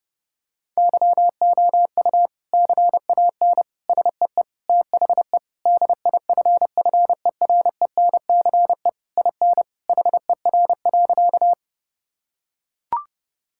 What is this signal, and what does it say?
Morse code, heard as a keyed tone, about 32 wpm, 700 Hz. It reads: YOU CAN SEE THE DIFFERENCE IN HER.
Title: Sentences from Top 1000 Words - Review Part 5 - 30wpm